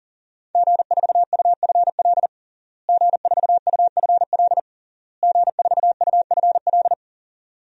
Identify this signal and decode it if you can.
Morse code, heard as a keyed tone, about 40 wpm, 700 Hz. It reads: G4UFL G4UFL G4UFL
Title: DXCC-100 Call Signs - 5 Characters - Repeated 3x - 40wpm